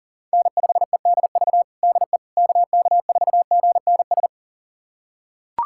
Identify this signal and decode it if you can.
Morse code, heard as a keyed tone, about 37 wpm, 700 Hz. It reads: N5EDV DE KK4GNS
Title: USA Callsign DE Callsign 40wpm